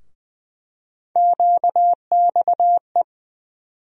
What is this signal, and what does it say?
Morse code, heard as a keyed tone, about 20 wpm, 700 Hz. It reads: QXE
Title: USA Call Sign Postfixes - 3 Letters 20wpm